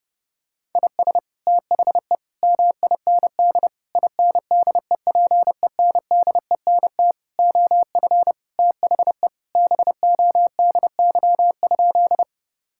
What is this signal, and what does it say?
Morse code, heard as a keyed tone, about 30 wpm, 700 Hz. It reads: IS THE MIND INDEPENDENT OF THE BODY?
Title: Sentences from Top 1000 Words - Review Part 6 - 30wpm